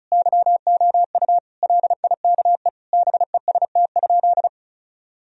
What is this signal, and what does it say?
Morse code, heard as a keyed tone, about 34 wpm, 700 Hz. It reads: YOU LIKE BEST?
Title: Sentences from Top 1000 Words - Review Part 2 - 35wpm